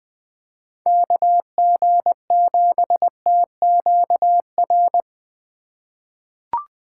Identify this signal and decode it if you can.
Morse code, heard as a keyed tone, about 21 wpm, 700 Hz. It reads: KG7TQR
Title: USA Call Signs - 2x Character Spacing - 20wpm